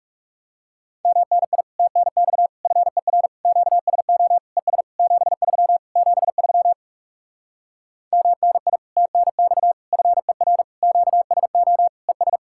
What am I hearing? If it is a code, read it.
Morse code, heard as a keyed tone, about 37 wpm, 700 Hz. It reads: MNI TNX FER QSO ES 73 73 MNI TNX FER QSO ES
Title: Speed-Racing Common QSO Phrases Encore - 30wpm